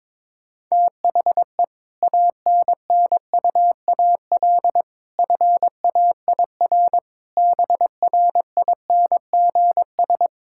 Morse code, heard as a keyed tone, about 22 wpm, 700 Hz. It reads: THE ANNUAL FAIR BRINGS
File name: Difficult Sentences with Hints - 22wpm